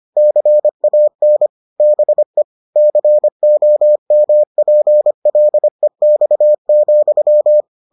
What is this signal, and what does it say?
Morse code, heard as a keyed tone, about 25 wpm, 600 Hz. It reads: CAN BE COMPLEX,